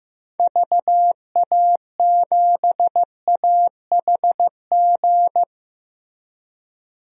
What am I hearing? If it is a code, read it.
Morse code, heard as a keyed tone, about 16 wpm, 700 Hz. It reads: VA7AHG
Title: DXCC-100 Call Signs - 6 Characters - Repeated 2x - 15wpm